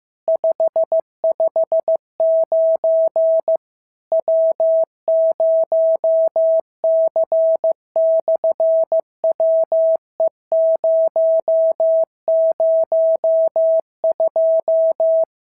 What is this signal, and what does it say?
Morse code, heard as a keyed tone, about 15 wpm, 650 Hz. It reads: 559 W0C/WE002
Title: SOTA QSO Practice 15wpm